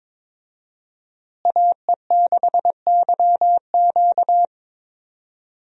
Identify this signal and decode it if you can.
Morse code, heard as a keyed tone, about 22 wpm, 700 Hz. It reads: AE6YQ
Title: USA Call Signs - 5 Characters - Repeated 3x - 22wpm